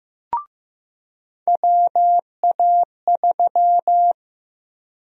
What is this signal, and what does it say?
Morse code, heard as a keyed tone, about 15 wpm, 700 Hz. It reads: WA3
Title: USA Call Sign Prefixes - 15wpm